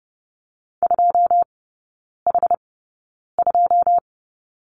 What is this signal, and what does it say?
Morse code, heard as a keyed tone, about 30 wpm, spaced as for 9 wpm, 700 Hz. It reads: 2H2
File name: 081 Lesson 14e - Number 2 - Three Characters 30wpm